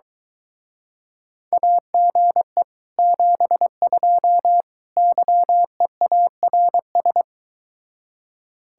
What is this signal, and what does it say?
Morse code, heard as a keyed tone, about 22 wpm, 700 Hz. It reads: AGE 72 YEARS